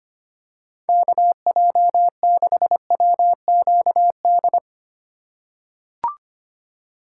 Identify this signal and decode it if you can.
Morse code, heard as a keyed tone, about 25 wpm, 700 Hz. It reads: KJ6WQD
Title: USA Call Signs - 2x Character Spacing - 25wpm